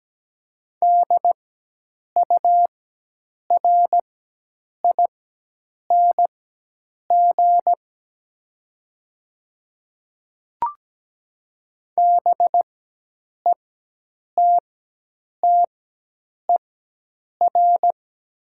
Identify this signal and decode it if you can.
Morse code, heard as a keyed tone, about 17 wpm, spaced as for 8 wpm, 700 Hz. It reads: DURING BETTER
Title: 4x Character Spacing - Top 500 Words - 17wpm